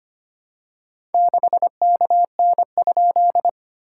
Morse code, heard as a keyed tone, about 25 wpm, 700 Hz. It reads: 6KN?
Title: SOTA QSO Practice 25wpm